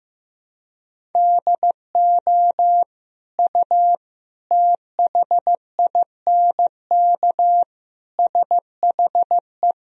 Morse code, heard as a keyed tone, about 15 wpm, 700 Hz. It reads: DO U THINK SHE
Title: Sentences from Top 200 Words and Abbreviations 15wpm